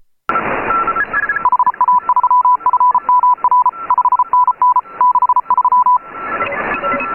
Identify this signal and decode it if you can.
Morse code, heard as a keyed tone, about 33 wpm, 1000 Hz. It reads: HA3FMR 5NN 63
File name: AUDIO2018 / HA3FMR CW